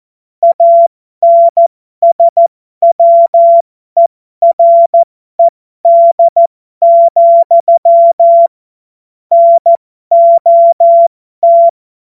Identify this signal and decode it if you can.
Morse code, heard as a keyed tone, about 14 wpm, spaced as for 12 wpm, 680 Hz. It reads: ANSWERED, NOT
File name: Morsekurs / Witze